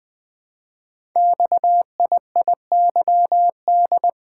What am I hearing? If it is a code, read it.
Morse code, heard as a keyed tone, about 20 wpm, 700 Hz. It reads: XIIYD
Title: Code Groups - 5 Letter:Numbers - 20wpm